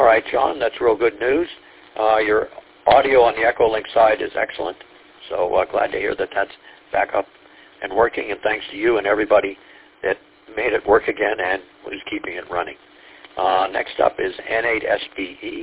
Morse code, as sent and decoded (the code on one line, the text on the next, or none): none